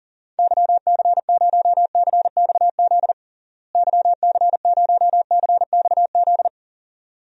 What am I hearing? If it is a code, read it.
Morse code, heard as a keyed tone, about 40 wpm, 700 Hz. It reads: YC0CXZ YC0CXZ
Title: DXCC-100 Call Signs - 6 Characters - Repeated 2x - 40wpm